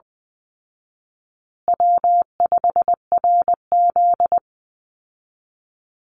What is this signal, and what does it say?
Morse code, heard as a keyed tone, about 20 wpm, 700 Hz. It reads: W5RZ
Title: Easy Call Signs Repeated 3x - 20wpm